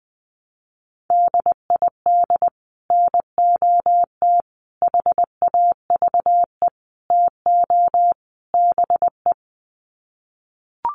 Morse code, heard as a keyed tone, about 20 wpm, 700 Hz. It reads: DID NOT HAVE TO BE
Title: Sets of 5 Words Encore2 20wpm